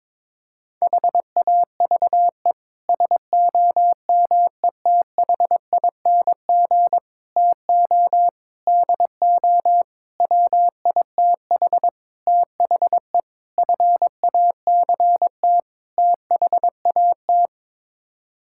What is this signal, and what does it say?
Morse code, heard as a keyed tone, about 22 wpm, 700 Hz. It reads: HAVE SOMETHING TO DO WITH THE FACT THAT
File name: practice / 22wpm / Sets of 8 Words - 22wpm